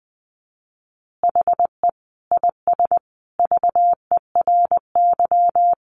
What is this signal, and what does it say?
Morse code, heard as a keyed tone, about 20 wpm, 700 Hz. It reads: HE IS VERY
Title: Easy Phrases 20wpm